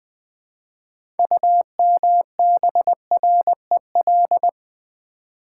Morse code, heard as a keyed tone, about 20 wpm, 700 Hz. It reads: UMBREL
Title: Build A Word - 2 Letters At A Time - 8-Letter Words - 20wpm